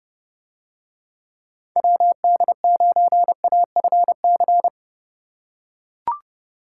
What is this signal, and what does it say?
Morse code, heard as a keyed tone, about 29 wpm, 700 Hz. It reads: WD9AFC